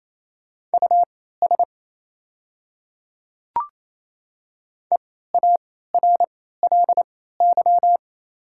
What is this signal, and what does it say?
Morse code, heard as a keyed tone, about 28 wpm, spaced as for 16 wpm, 700 Hz. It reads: US EARLY
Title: 3x Character Spacing - Top 500 Words - 28wpm